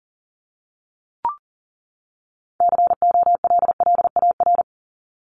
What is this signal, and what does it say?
Morse code, heard as a keyed tone, about 40 wpm, 700 Hz. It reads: COLLAR
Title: Top 1751-2000 Words 40wpm